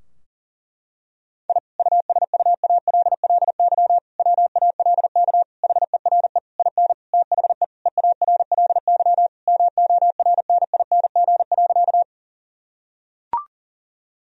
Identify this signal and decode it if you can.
Morse code, heard as a keyed tone, about 40 wpm, 700 Hz. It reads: I USUALLY WALK HERE IN THE EARLY MORNING.
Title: Sentences from Top 1000 Words - Review Part 3 - 40wpm